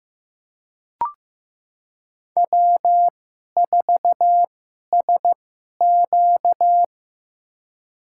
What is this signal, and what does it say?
Morse code, heard as a keyed tone, about 15 wpm, spaced as for 11 wpm, 700 Hz. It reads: W4SQ
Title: USA Call Signs - 4 Characters - 2x Character Spacing - 15wpm